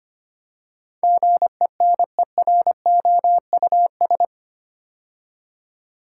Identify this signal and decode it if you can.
Morse code, heard as a keyed tone, about 25 wpm, 700 Hz. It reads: GENEROUS
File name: 8 Letter Words - First 3 Letters Given - 25wpm